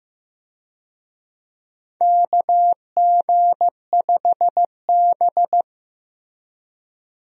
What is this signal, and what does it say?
Morse code, heard as a keyed tone, about 15 wpm, 700 Hz. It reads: KG5B
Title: USA Call Signs - 4 Characters - Repeated 2x - 15wpm